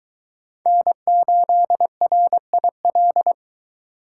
Morse code, heard as a keyed tone, about 22 wpm, 700 Hz. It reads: N8RIL